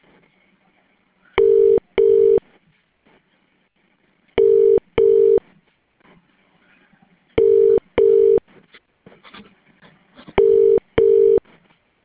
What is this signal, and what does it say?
Ringback tone (UK), 400 + 450 Hz.